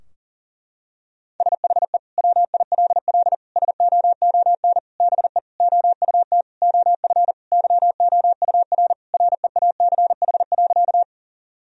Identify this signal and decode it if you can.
Morse code, heard as a keyed tone, about 37 wpm, 700 Hz. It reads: SHE WILL SOON BE OUT OF YOUR REACH.
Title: Sentences from Top 1000 Words - Review Part 2 - 40wpm